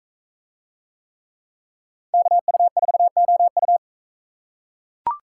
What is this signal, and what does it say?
Morse code, heard as a keyed tone, about 38 wpm, 700 Hz. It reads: KU4OU